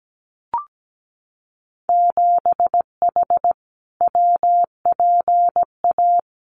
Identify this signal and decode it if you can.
Morse code, heard as a keyed tone, about 17 wpm, 700 Hz. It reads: 7H WPA
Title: Winter Field Day Exchanges Encore 17wpm